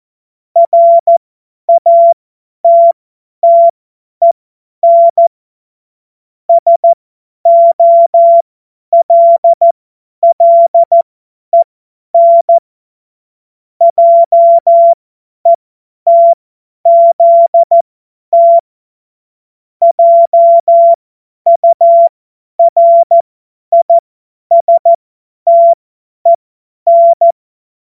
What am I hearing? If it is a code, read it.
Morse code, heard as a keyed tone, about 14 wpm, spaced as for 10 wpm, 680 Hz. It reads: RATTEN SOLLEN JETZT JURISTEN
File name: Morsekurs / Witze / Witz 02. August 2022 50 F70